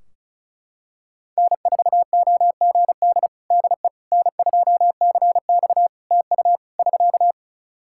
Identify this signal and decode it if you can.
Morse code, heard as a keyed tone, about 35 wpm, 700 Hz. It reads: N4OGD DE N2CX TU <SK>
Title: Practice QSO - Exchange 4 - Ending - 35wpm